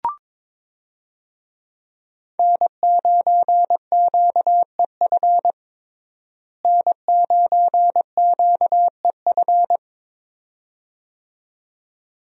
Morse code, heard as a keyed tone, about 22 wpm, 700 Hz. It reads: N9QEF N9QEF
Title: USA Call Signs - 5 Characters - Repeated 2x - 22wpm